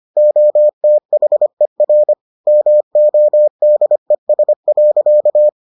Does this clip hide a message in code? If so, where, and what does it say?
Yes: Morse code from 0.2 to 5.5 s, heard as a keyed tone, about 25 wpm, 600 Hz. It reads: OTHER MODES.